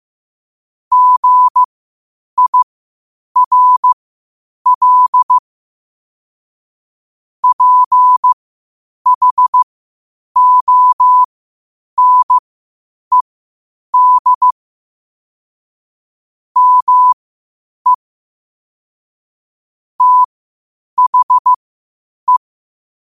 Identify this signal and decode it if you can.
Morse code, heard as a keyed tone, about 15 wpm, spaced as for 8 wpm, 1000 Hz. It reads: GIRL PHONED ME THE
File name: qotdfiles / 8wpm